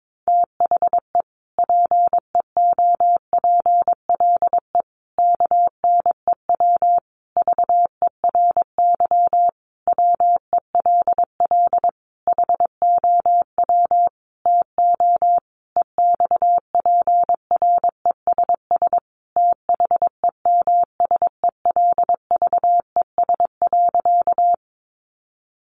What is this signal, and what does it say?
Morse code, heard as a keyed tone, about 22 wpm, 700 Hz. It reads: THE PEOPLE KNEW VERY WELL HOW TO EXPRESS THEMSELVES.
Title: Sentences from Top 1100 Words 22wpm